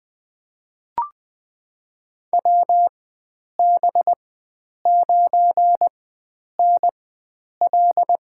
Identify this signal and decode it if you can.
Morse code, heard as a keyed tone, about 20 wpm, spaced as for 9 wpm, 700 Hz. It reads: WB9NL
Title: USA Call Signs - 4x Character Spacing - 20wpm